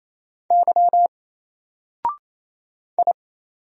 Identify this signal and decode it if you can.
Morse code, heard as a keyed tone, about 27 wpm, 700 Hz. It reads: Y I
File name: Single-Letter-Number-28wpm